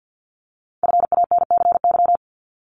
Morse code, heard as a keyed tone, about 50 wpm, 690 Hz. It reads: FANCY
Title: Multi-letter Phonemes Repeated 3x 50wpm